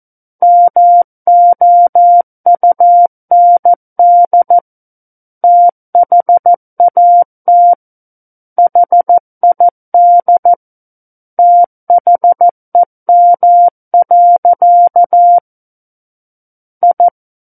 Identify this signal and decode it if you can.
Morse code, heard as a keyed tone, about 14 wpm, 700 Hz. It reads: MOUND THAT HID THEM. I
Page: War of the Worlds - 05-Chapter 5 - 14 WPM